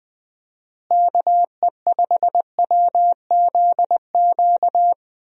Morse code, heard as a keyed tone, about 20 wpm, 700 Hz. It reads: KE5WZQ
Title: USA Call Signs - 6 Characters - Repeated 2x - 20wpm